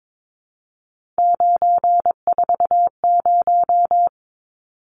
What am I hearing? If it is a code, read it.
Morse code, heard as a keyed tone, about 22 wpm, 700 Hz. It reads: 940
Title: Numbers - 3-Digit 22wpm